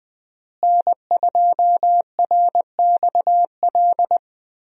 Morse code, heard as a keyed tone, about 20 wpm, 700 Hz. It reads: N2RXL
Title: USA Call Signs - 5 Characters - Repeated 2x - 20wpm